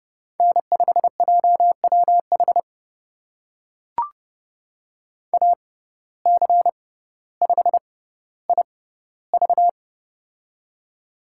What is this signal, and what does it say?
Morse code, heard as a keyed tone, about 30 wpm, spaced as for 15 wpm, 700 Hz. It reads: N5JWH AC5IV
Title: USA Call Signs - 5 Characters - 6x Character Spacing - 30wpm